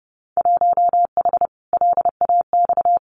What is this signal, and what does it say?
Morse code, heard as a keyed tone, about 31 wpm, 700 Hz. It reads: 1H LAX
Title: Winter Field Day Exchanges 30wpm